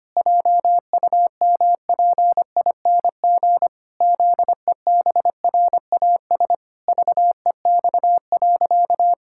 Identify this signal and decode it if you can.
Morse code, heard as a keyed tone, about 25 wpm, 700 Hz. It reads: JUMPING ZEBRAS VEX.
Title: Pangram Challenge v2 - 25wpm